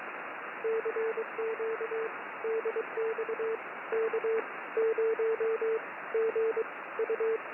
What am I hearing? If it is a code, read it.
Morse code, heard as a keyed tone, about 22 wpm, spaced as for 17 wpm, 450 Hz, with radio noise. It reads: CQ DX K0GU
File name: K0GU